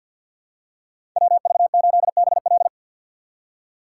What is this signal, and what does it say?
Morse code, heard as a keyed tone, about 50 wpm, 700 Hz. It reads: WV8BL
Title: USA Call Signs - 5 Characters - 4x Character Spacing - 50wpm